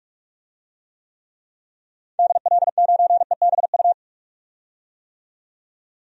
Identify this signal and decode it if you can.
Morse code, heard as a keyed tone, about 46 wpm, 700 Hz. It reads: DL9EBU
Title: DXCC-100 Call Signs - 6 Characters - 45wpm